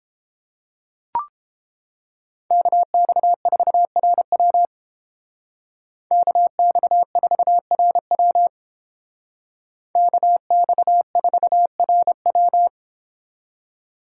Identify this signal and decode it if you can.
Morse code, heard as a keyed tone, about 28 wpm, 700 Hz. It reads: KX4RW KX4RW KX4RW